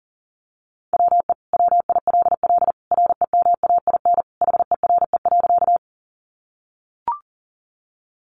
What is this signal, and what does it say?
Morse code, heard as a keyed tone, about 40 wpm, 700 Hz. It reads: WE WILL REMAIN HERE.
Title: Sentences from Top 1200 Words 40wpm